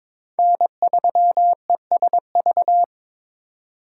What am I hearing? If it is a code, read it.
Morse code, heard as a keyed tone, about 26 wpm, 700 Hz. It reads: N3ESV
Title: Speed Racing - USA 1x3 Call Signs 22wpm